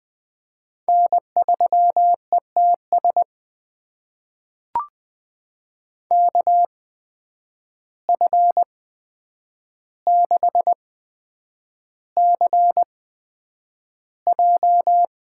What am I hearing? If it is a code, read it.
Morse code, heard as a keyed tone, about 20 wpm, spaced as for 9 wpm, 700 Hz. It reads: N3ETS KF6CJ
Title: USA Call Signs - 5 Characters - 8x Character Spacing - 20wpm